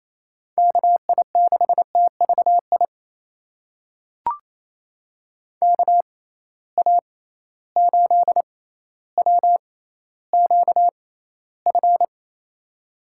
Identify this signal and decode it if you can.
Morse code, heard as a keyed tone, about 28 wpm, spaced as for 14 wpm, 700 Hz. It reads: KI6TVI KA8WQF
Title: USA Call Signs - 6 Characters - 6x Character Spacing - 28wpm